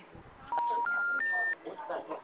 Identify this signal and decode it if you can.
Special information tone, with speech.
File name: out- -2016